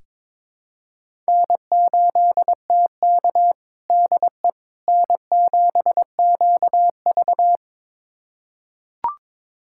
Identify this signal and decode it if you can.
Morse code, heard as a keyed tone, about 21 wpm, 700 Hz. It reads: N8TK DE N7QV
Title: USA Callsign DE Callsign - 4 Characters - 22wpm